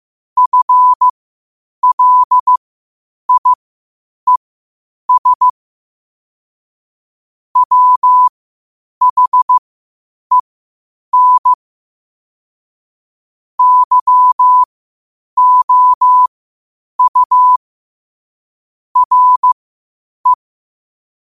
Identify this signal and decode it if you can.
Morse code, heard as a keyed tone, about 15 wpm, spaced as for 8 wpm, 1000 Hz. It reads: FLIES WHEN YOU RE
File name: qotdfiles / 8wpm